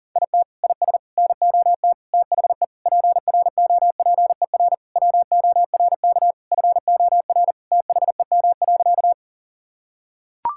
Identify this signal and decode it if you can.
Morse code, heard as a keyed tone, about 40 wpm, 700 Hz. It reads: IT IS NOT THE PROPER WORK FOR THEM.
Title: Sentences from Top 1000 Words - Review Part 5 - 40wpm